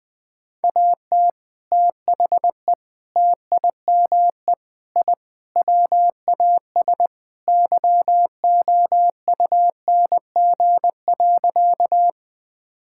Morse code, heard as a keyed tone, about 20 wpm, 700 Hz. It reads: AT THE TIME I WAS YOUNG.
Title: Sentences with 3-Word Phrases - 20wpm